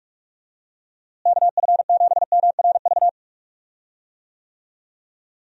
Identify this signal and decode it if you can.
Morse code, heard as a keyed tone, about 43 wpm, 700 Hz. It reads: KF7MRV
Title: USA Call Signs - 6 Characters - Repeated 2x - 45wpm